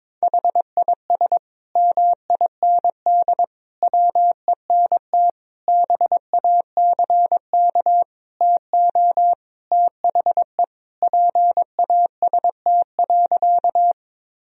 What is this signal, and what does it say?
Morse code, heard as a keyed tone, about 22 wpm, 700 Hz. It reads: HIS MIND WENT BACK TO THE PAST.
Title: Sentences from Top 1000 Words - Review Part 3 - 22wpm